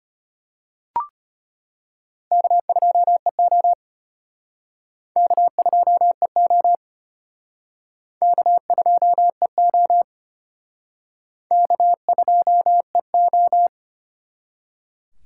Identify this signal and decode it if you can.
Morse code, heard as a keyed tone, about 35 wpm, 700 Hz. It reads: K2EO K2EO K2EO K2EO